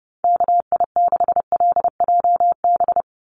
Morse code, heard as a keyed tone, about 30 wpm, 700 Hz. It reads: KI6LJB
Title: USA Call Signs 30wpm